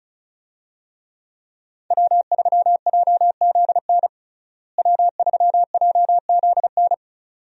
Morse code, heard as a keyed tone, about 34 wpm, 700 Hz. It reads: W3JZN W3JZN
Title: USA Call Signs - 5 Characters - Repeated 2x - 35wpm